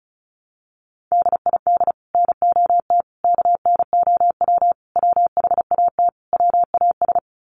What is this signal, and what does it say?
Morse code, heard as a keyed tone, about 34 wpm, 700 Hz. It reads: DID NOT KNOW WHAT WAS
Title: Sets of 5 Words Encore2 35wpm